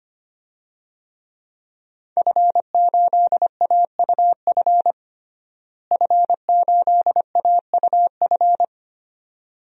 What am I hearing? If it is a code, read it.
Morse code, heard as a keyed tone, about 25 wpm, 700 Hz. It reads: F8AUF F8AUF
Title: DXCC-100 Call Signs - 5 Characters - Repeated 2x - 25wpm